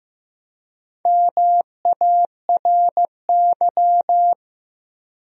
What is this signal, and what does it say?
Morse code, heard as a keyed tone, about 15 wpm, 700 Hz. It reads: MARY
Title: USA Common Names - Max 4 Letters - 15wpm